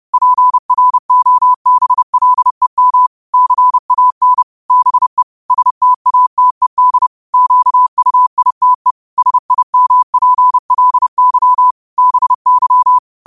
Morse code, heard as a keyed tone, about 30 wpm, 1000 Hz. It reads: PROBLEM CAN BE STATED QUITE SIMPLY BY
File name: qotdfiles / 30wpm